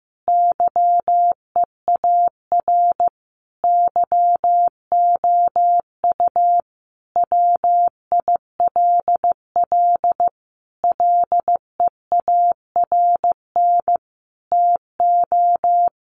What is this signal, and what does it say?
Morse code, heard as a keyed tone, about 14 wpm, 700 Hz. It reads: YEAR YOU WILL LEARN TO
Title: Sentences from Top 600 Words 15wpm